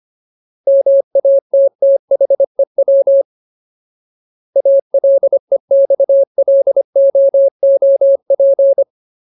Morse code, heard as a keyed tone, about 25 wpm, 550 Hz. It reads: MATTHEW ALEXLOOP
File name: Ultimate Word List / 25WPM Ultimate